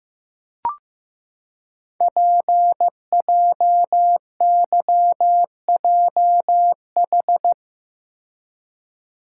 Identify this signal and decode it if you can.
Morse code, heard as a keyed tone, about 15 wpm, 700 Hz. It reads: PJYJH
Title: Code Groups - 5 Letters - 15wpm